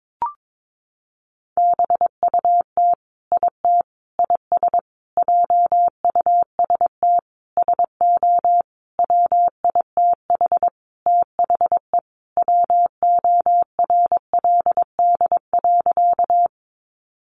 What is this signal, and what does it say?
Morse code, heard as a keyed tone, about 22 wpm, 700 Hz. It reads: BUT IT IS JUST SO WITH THE WORLD.
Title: Sentences from Top 200 Words 22wpm